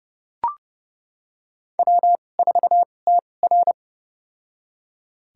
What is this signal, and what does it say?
Morse code, heard as a keyed tone, about 31 wpm, spaced as for 22 wpm, 700 Hz. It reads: W4TR
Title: USA Call Signs - 4 Characters - 2x Character Spacing - 30wpm